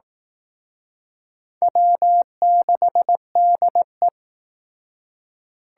Morse code, heard as a keyed tone, about 18 wpm, 700 Hz. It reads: W6DE
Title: Speed Racing - USA 1x2 Call Signs 15wpm